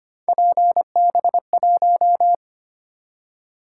Morse code, heard as a keyed tone, about 25 wpm, 700 Hz. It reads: PB1
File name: Three Letter-Number Combinations - 25wpm